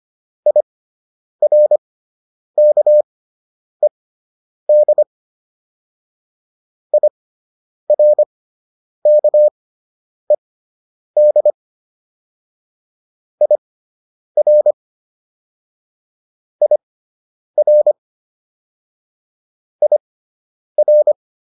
Morse code, heard as a keyed tone, about 25 wpm, spaced as for 9 wpm, 600 Hz. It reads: IRKED IRKED IR IR IR